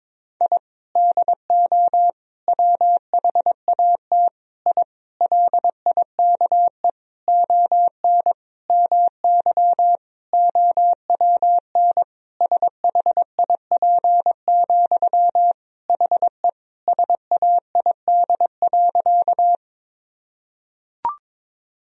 Morse code, heard as a keyed tone, about 22 wpm, 700 Hz. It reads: I DO WHAT I LIKE ON MY OWN SHIP, HE SAID.
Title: Sentences from Top 1000 Words - Review Part 2 - 22wpm